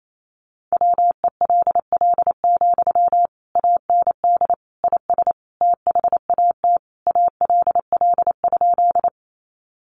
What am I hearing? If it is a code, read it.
Morse code, heard as a keyed tone, about 28 wpm, 700 Hz. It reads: WELL, AND IS THAT ALL?
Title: Sentences from Top 200 Words 28wpm